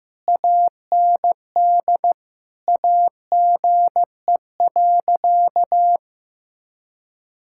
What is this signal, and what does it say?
Morse code, heard as a keyed tone, about 16 wpm, 700 Hz. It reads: AND AGE.
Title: Sentences from Top 1600 Words 15wpm